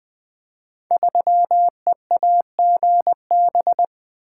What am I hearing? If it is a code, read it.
Morse code, heard as a keyed tone, about 20 wpm, 700 Hz. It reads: 3EAGB
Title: Code Groups - 5 Letter:Numbers - 20wpm